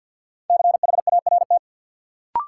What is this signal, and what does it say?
Morse code, heard as a keyed tone, about 51 wpm, 700 Hz. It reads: CHART
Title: Sentences from Top 1300 Words - New Word Drill Alt - 50wpm